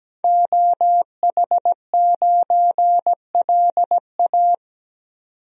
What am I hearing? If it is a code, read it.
Morse code, heard as a keyed tone, about 17 wpm, 700 Hz. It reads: OH9LA
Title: DXCC-100 Call Signs - 5 Characters - 17wpm